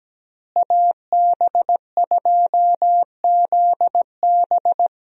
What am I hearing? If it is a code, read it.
Morse code, heard as a keyed tone, about 17 wpm, 700 Hz. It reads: AB2ZB
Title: USA Call Signs - 5 Characters - Repeated 3x - 17wpm